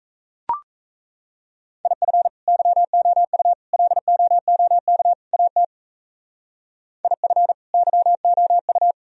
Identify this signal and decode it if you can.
Morse code, heard as a keyed tone, about 32 wpm, 700 Hz. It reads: IF YOU LOOK AT IF YOU
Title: Speed-Racing - Sets of 4 Words 28wpm